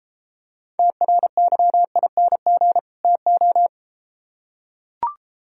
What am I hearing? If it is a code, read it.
Morse code, heard as a keyed tone, about 33 wpm, 700 Hz. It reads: TRYING TO